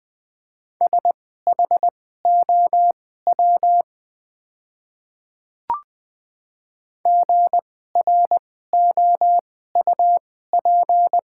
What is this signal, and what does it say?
Morse code, heard as a keyed tone, about 20 wpm, spaced as for 15 wpm, 700 Hz. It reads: SHOW GROUP
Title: 2x Character Spacing - Top 300 Words - 20wpm